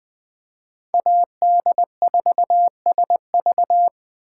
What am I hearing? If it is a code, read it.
Morse code, heard as a keyed tone, about 20 wpm, 700 Hz. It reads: AD4SV
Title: Speed Racing - USA Call Signs 17wpm